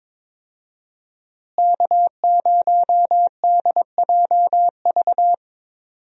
Morse code, heard as a keyed tone, about 22 wpm, 700 Hz. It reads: K0DJV
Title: USA Call Signs - 5 Characters - Repeated 2x - 22wpm